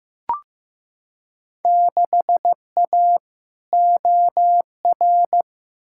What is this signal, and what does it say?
Morse code, heard as a keyed tone, about 15 wpm, 700 Hz. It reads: 6A OR
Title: Field Day Exchanges - 15wpm